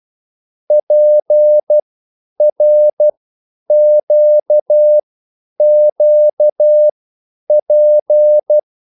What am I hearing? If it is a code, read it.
Morse code, heard as a keyed tone, about 12 wpm, spaced as for 9 wpm, 600 Hz. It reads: PRQQP
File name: QSL / CW lesson